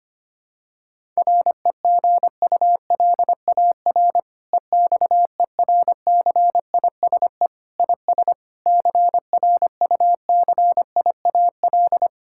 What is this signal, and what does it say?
Morse code, heard as a keyed tone, about 25 wpm, 700 Hz. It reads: REGULAR EXERCISE IS CRUCIAL
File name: practice / 25wpm / Difficult Sentences with Hints - 25wpm